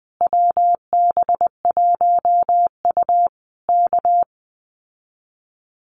Morse code, heard as a keyed tone, about 20 wpm, 700 Hz. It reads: WB1U K
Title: Common QSO Phrases Encore 20wpm